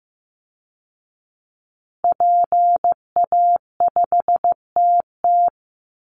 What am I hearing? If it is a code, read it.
Morse code, heard as a keyed tone, about 15 wpm, 700 Hz. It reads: PA5TT
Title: Call Signs Repeated 2x - 15wpm